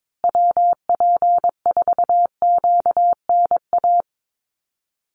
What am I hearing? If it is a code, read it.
Morse code, heard as a keyed tone, about 22 wpm, 700 Hz. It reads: WP4QNA
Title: USA Call Signs Repeated 3x 22wpm